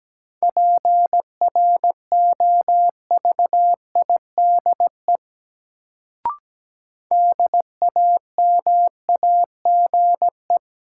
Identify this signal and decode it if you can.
Morse code, heard as a keyed tone, about 17 wpm, 700 Hz. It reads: PROVIDE DAMAGE
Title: Sentences from Top 1100 Words - New Word Drill Alt - 17wpm